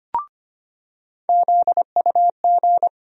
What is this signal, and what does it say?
Morse code, heard as a keyed tone, about 25 wpm, 700 Hz. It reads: ZUG
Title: USA Call Sign Postfixes - 3 Letters 25wpm